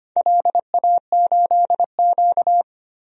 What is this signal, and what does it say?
Morse code, heard as a keyed tone, about 25 wpm, 700 Hz. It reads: LA8Q
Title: DXCC-100 Call Signs - 4 Characters - 25wpm